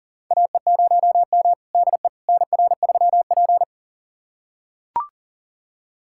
Morse code, heard as a keyed tone, about 40 wpm, 700 Hz. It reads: AE0M DE NR3P
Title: USA Callsign DE Callsign - 4 Characters - 40wpm